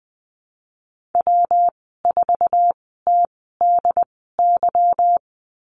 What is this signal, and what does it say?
Morse code, heard as a keyed tone, about 20 wpm, spaced as for 15 wpm, 700 Hz. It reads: W4TDY
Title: 186 Lesson 30g - Letter b - Call Signs 20wpm